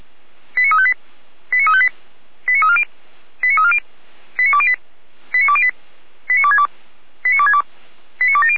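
Five-tone selective call (ZVEI), heard as quick runs of five tones, tones of about 80 ms. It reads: 89358 89358 89340 89340 89208 89208 89262 89262 89270
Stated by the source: ZVEI five-tone call received on an SDR.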